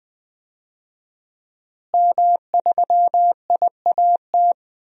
Morse code, heard as a keyed tone, about 20 wpm, 700 Hz. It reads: M3IAT
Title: DXCC-100 Call Signs - 5 Characters - Repeated 2x - 20wpm